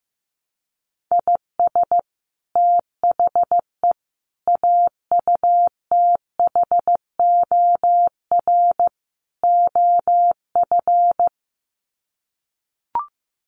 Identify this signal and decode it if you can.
Morse code, heard as a keyed tone, about 15 wpm, 700 Hz. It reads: IS THE AUTHOR OF
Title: Sets of 4 Words 15wpm